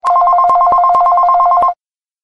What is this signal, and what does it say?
There is no code.